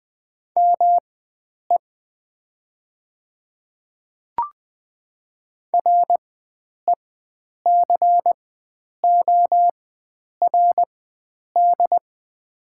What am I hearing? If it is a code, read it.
Morse code, heard as a keyed tone, about 20 wpm, spaced as for 9 wpm, 700 Hz. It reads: ME RECORD
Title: 4x Character Spacing - Top 500 Words - 20wpm